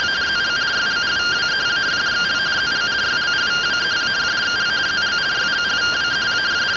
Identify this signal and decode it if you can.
Two-tone teleprinter signal, about 170 Hz shift.
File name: findings / CT AXIAL -142.gen-11 rtty rx